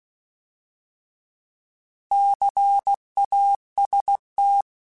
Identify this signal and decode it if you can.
Morse code, heard as a keyed tone, about 16 wpm, 800 Hz. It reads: CAST
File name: Alphabetical / Alphabetical mp3s / Cs @15